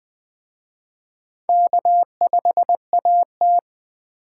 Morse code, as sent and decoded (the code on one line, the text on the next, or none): -.- ..... .- -
K5AT